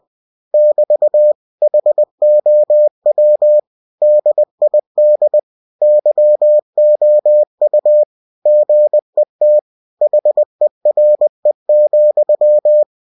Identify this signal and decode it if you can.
Morse code, heard as a keyed tone, about 20 wpm, 600 Hz. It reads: <BT> HOW DID YOU GET HERE,
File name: cw-books / winnie-the-pooh / winnie-20wpm-page-0098